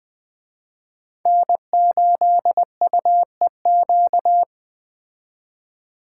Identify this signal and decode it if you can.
Morse code, heard as a keyed tone, about 21 wpm, 700 Hz. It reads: N8UEQ